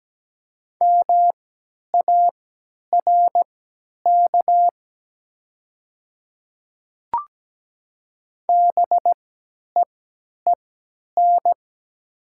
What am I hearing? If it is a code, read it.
Morse code, heard as a keyed tone, about 17 wpm, spaced as for 10 wpm, 700 Hz. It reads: MARK BEEN
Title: 3x Character Spacing - Top 300 Words - 17wpm